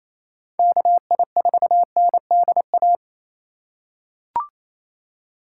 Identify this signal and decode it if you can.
Morse code, heard as a keyed tone, about 28 wpm, 700 Hz. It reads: KI4NDA
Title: USA Call Signs - 6 Characters - 2x Character Spacing - 28wpm